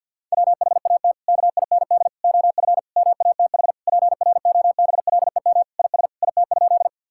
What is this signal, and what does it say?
Morse code, heard as a keyed tone, about 51 wpm, 700 Hz. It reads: WHAT KIND OF MATH PROBLEM IS IT?
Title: Sentences from Top 1000 Words - Review Part 6 - 50wpm